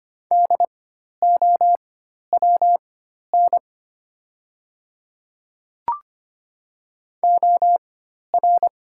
Morse code, heard as a keyed tone, about 25 wpm, spaced as for 12 wpm, 700 Hz. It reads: DOWN OR
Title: 4x Character Spacing - Top 100 Words - 25wpm